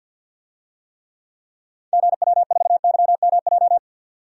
Morse code, heard as a keyed tone, about 53 wpm, 700 Hz. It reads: GW4YMJ
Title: DXCC-100 Call Signs - 6 Characters - 50wpm